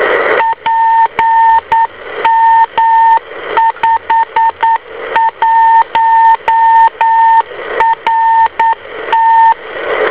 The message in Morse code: PM51RT